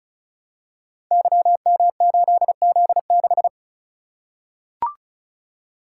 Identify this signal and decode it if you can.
Morse code, heard as a keyed tone, about 36 wpm, 700 Hz. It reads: YM8Z6
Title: Code Groups - 5 Letter:Numbers - 35wpm